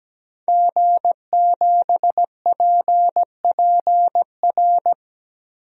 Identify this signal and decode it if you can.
Morse code, heard as a keyed tone, about 17 wpm, 700 Hz. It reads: G7PPR
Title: DXCC-100 Call Signs - 5 Characters - 17wpm